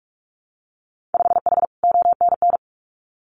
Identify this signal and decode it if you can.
Morse code, heard as a keyed tone, about 46 wpm, 700 Hz. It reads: 5H ONN
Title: Winter Field Day Exchanges Encore 45wpm